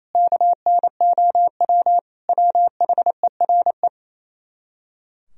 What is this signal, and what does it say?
Morse code, heard as a keyed tone, about 29 wpm, 700 Hz. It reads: KNOW WHERE
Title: Speed-Racing - Sets of 5 Words 28wpm